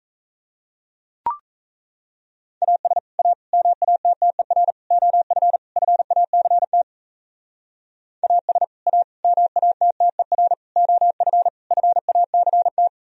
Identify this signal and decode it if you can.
Morse code, heard as a keyed tone, about 37 wpm, 700 Hz. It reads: AS A MATTER OF FACT AS A MATTER OF FACT